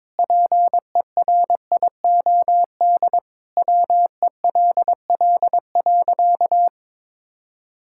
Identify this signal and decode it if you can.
Morse code, heard as a keyed tone, about 22 wpm, 700 Hz. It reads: PERIOD WELL.
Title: Sentences from Top 1000 Words - Review Part 5 - 22wpm